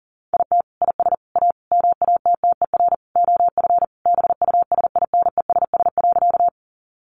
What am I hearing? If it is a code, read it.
Morse code, heard as a keyed tone, about 40 wpm, 700 Hz. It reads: IT IS A MATTER OF BUSINESS.
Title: Sentences from Top 700 Words 40wpm